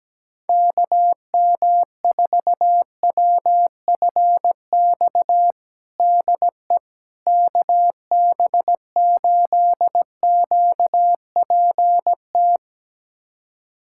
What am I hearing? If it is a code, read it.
Morse code, heard as a keyed tone, about 17 wpm, 700 Hz. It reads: KM4WFX DE KB8QPT
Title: USA Callsign DE Callsign - 6 Characters - 17wpm